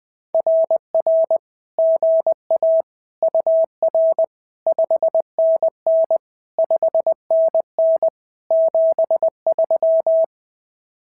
Morse code, heard as a keyed tone, about 20 wpm, 650 Hz. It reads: RR GA UR 5NN 5NN 73
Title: SOTA QSO Practice 20wpm